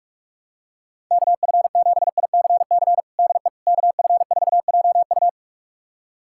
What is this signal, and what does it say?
Morse code, heard as a keyed tone, about 46 wpm, 700 Hz. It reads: KF7ICC DE KF4JU
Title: USA Callsign DE Callsign 45wpm